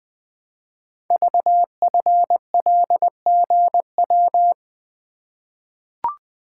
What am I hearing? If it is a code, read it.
Morse code, heard as a keyed tone, about 20 wpm, 700 Hz. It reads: VFLGW
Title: Code Groups - 5 Letters - 20wpm